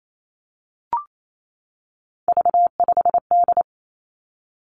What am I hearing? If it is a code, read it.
Morse code, heard as a keyed tone, about 28 wpm, 700 Hz. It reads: V5D
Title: Three Challenging Letter-Number Combinations 28wpm